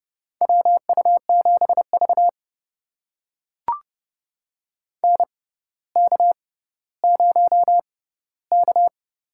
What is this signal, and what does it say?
Morse code, heard as a keyed tone, about 30 wpm, spaced as for 15 wpm, 700 Hz. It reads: WU7V NK0K
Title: USA Call Signs - 6x Character Spacing - 30wpm